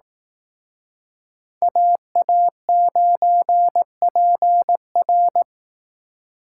Morse code, heard as a keyed tone, about 18 wpm, 700 Hz. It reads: AA9PR
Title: Speed Racing - USA 2x2 Call Signs 15wpm